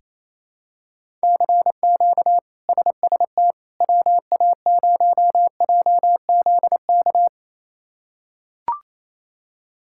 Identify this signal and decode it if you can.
Morse code, heard as a keyed tone, about 28 wpm, 700 Hz. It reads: CQ SST WA0JZK
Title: K1USN SST QSO Practice - 28wpm